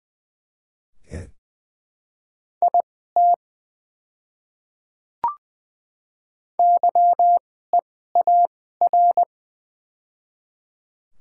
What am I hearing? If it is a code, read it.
Morse code, heard as a keyed tone, about 21 wpm, spaced as for 15 wpm, 690 Hz. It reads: IT YEAR
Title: 2x Character Spacing - Top 200 Words - 20wpm